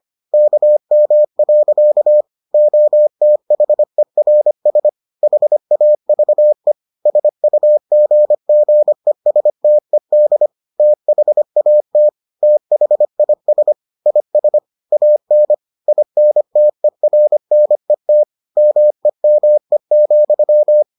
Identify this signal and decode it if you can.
Morse code, heard as a keyed tone, about 25 wpm, 600 Hz. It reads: KM. OTHERS HAVE SUGGESTED THAT THIS IS AN INTERNET MEME,